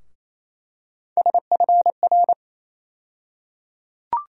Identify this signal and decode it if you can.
Morse code, heard as a keyed tone, about 28 wpm, 700 Hz. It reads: SFR
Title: USA Call Sign Postfixes - 3 Letters 28wpm